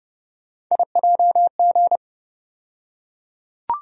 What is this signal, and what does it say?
Morse code, heard as a keyed tone, about 31 wpm, 700 Hz. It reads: IJG
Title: USA Call Sign Postfixes - 3 Letters 30wpm